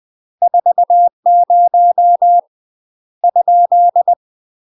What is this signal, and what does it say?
Morse code, heard as a keyed tone, about 20 wpm, 700 Hz. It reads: 40 ?